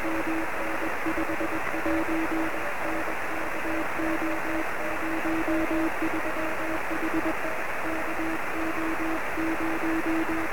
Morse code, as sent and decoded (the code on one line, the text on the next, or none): none